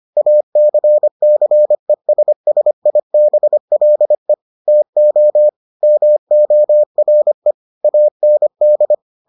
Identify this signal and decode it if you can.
Morse code, heard as a keyed tone, about 25 wpm, 600 Hz. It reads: ACCESSIBLE TO MORE AND